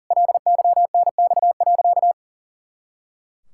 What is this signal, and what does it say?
Morse code, heard as a keyed tone, about 40 wpm, 700 Hz. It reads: LYNX.
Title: Pangram Challenge v2 - 40wpm